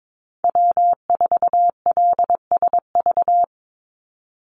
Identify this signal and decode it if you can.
Morse code, heard as a keyed tone, about 22 wpm, 700 Hz. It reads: W4LSV
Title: Call Signs Repeated 3x - 22wpm